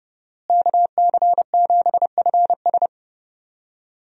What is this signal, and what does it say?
Morse code, heard as a keyed tone, about 30 wpm, 700 Hz. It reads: KC7FS
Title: USA Call Signs - 5 Characters - Repeated 3x - 30wpm